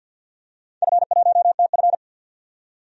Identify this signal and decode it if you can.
Morse code, heard as a keyed tone, about 52 wpm, 700 Hz. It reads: F1TF
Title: DXCC-100 Call Signs - 4 Characters - 50wpm